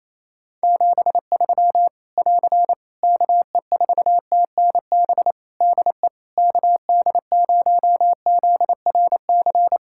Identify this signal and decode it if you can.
Morse code, heard as a keyed tone, about 28 wpm, 700 Hz. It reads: 73 <AR> KE4TNB DE KD0ZRC
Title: K1USN SST QSO Practice 28wpm